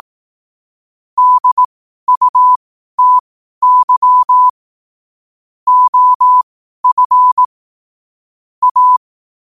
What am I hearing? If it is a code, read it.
Morse code, heard as a keyed tone, about 18 wpm, spaced as for 12 wpm, 1000 Hz. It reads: DUTY OF A
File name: qotdfiles / 12wpm